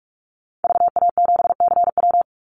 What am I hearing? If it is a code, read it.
Morse code, heard as a keyed tone, about 44 wpm, 700 Hz. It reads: VA7CW
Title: Call Signs - 45wpm